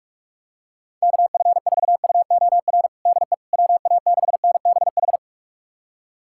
Morse code, heard as a keyed tone, about 46 wpm, 700 Hz. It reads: KF4UOR DE WA6NBH
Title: USA Callsign DE Callsign - 6 Characters - 45wpm